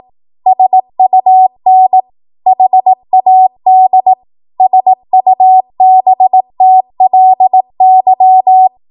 Morse code, heard as a keyed tone, about 18 wpm, 750 Hz. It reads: SUN HAD SUBTLY
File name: Morse / 18WPM